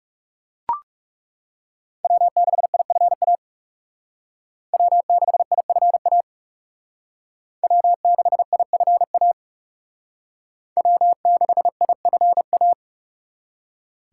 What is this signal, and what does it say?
Morse code, heard as a keyed tone, about 38 wpm, 700 Hz. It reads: W6IFA W6IFA W6IFA W6IFA